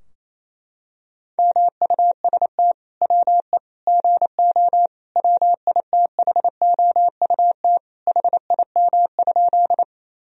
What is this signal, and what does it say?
Morse code, heard as a keyed tone, about 28 wpm, 700 Hz. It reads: MUST WE GO WITHOUT HIM?
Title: Sentences from Top 1000 Words - Review Part 3 - 28wpm